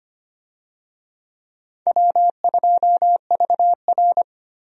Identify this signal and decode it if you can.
Morse code, heard as a keyed tone, about 25 wpm, 700 Hz. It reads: W2VR
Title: USA Call Signs - 4 Characters - 25wpm